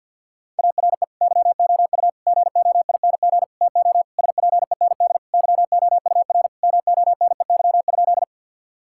Morse code, heard as a keyed tone, about 50 wpm, 700 Hz. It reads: ARE YOU GOING TO SPEND YOUR MONEY?
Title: Sentences from Top 1000 Words - Review Part 5 - 50wpm